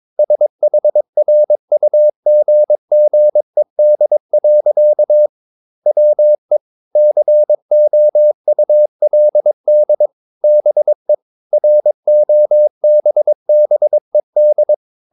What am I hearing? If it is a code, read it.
Morse code, heard as a keyed tone, about 22 wpm, 600 Hz. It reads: SHRUGGED. WE COULD BE ROBBED
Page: Romanen Never af Ken Follet på engelsk.